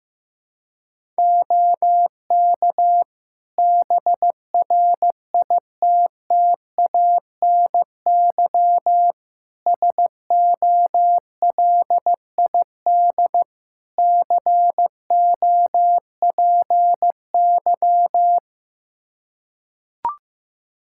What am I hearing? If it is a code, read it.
Morse code, heard as a keyed tone, about 15 wpm, 700 Hz. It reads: OK BRITTANY SOLID COPY
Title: K1USN SST QSO Practice 15wpm